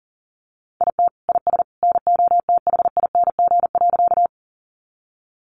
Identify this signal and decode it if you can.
Morse code, heard as a keyed tone, about 40 wpm, 700 Hz. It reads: IT IS NOTHING.
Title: Sentences from Top 500 Words 40wpm